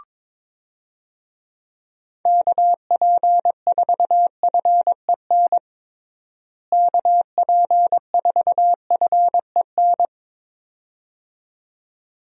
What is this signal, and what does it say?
Morse code, heard as a keyed tone, about 22 wpm, 700 Hz. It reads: KP4FEN KP4FEN
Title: DXCC-100 Call Signs - 6 Characters - Repeated 2x - 22wpm